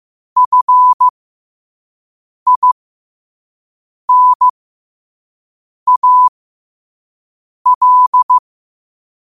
Morse code, heard as a keyed tone, about 15 wpm, spaced as for 5 wpm, 1000 Hz. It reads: FINAL
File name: qotdfiles / 5wpm